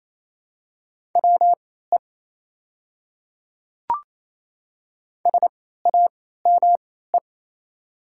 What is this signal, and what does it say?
Morse code, heard as a keyed tone, about 28 wpm, spaced as for 16 wpm, 700 Hz. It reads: WE SAME
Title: 3x Character Spacing - Top 300 Words - 28wpm